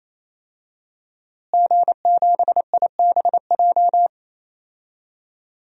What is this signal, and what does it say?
Morse code, heard as a keyed tone, about 27 wpm, 700 Hz. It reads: G7IBJ
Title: DXCC-100 Call Signs - 5 Characters - Repeated 3x - 28wpm